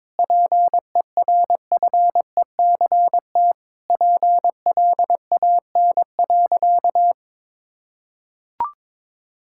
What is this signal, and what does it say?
Morse code, heard as a keyed tone, about 22 wpm, 700 Hz. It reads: PERFECT PLAN.
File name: Sentences with 7-Word Phrases - 22wpm